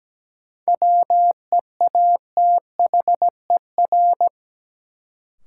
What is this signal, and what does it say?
Morse code, heard as a keyed tone, about 17 wpm, 700 Hz. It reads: WEATHER
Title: Sentences from Top 800 Words - New Word Drill - 17wpm